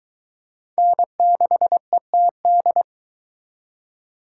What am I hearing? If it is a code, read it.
Morse code, heard as a keyed tone, about 22 wpm, 700 Hz. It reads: N6ETD